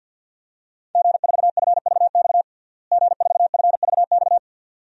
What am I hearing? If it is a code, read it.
Morse code, heard as a keyed tone, about 52 wpm, 700 Hz. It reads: G4FVX G4FVX
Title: DXCC-100 Call Signs - 5 Characters - Repeated 2x - 50wpm